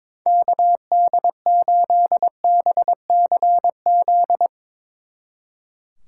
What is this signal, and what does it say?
Morse code, heard as a keyed tone, about 22 wpm, 700 Hz. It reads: KD8BCZ
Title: Speed Racing - USA Call Signs 22wpm